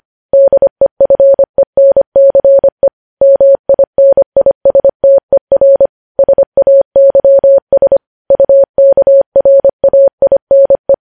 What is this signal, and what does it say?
Morse code, heard as a keyed tone, about 25 wpm, 550 Hz. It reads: DEFENCE MINISTER SAYS UKRAINE
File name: podcast / News headlines-1x-25x25